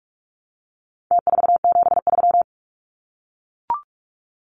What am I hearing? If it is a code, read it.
Morse code, heard as a keyed tone, about 44 wpm, 700 Hz. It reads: T473
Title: Numbers - Contest Style - 4-Digit 45wpm